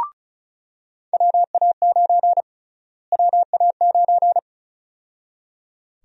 Morse code, heard as a keyed tone, about 36 wpm, 700 Hz. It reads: WA9 WA9
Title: USA Call Sign Prefixes - Repeated 2x - 35wpm